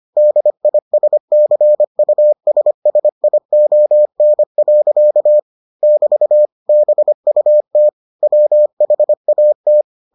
Morse code, heard as a keyed tone, about 25 wpm, 600 Hz. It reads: DISCUSSION. <BT> BUT WHAT